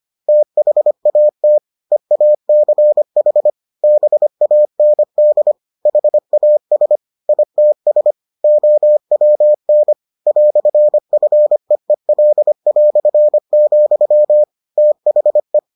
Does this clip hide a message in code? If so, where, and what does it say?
Yes: Morse code from 0.3 to 15.6 s, heard as a keyed tone, about 25 wpm, 600 Hz. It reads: THAT EACH BAND HAS ITS OWN "FEEL", THE